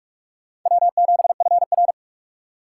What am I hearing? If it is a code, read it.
Morse code, heard as a keyed tone, about 59 wpm, spaced as for 52 wpm, 700 Hz. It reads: W7FR
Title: Speed Racing - USA Call Signs 45wpm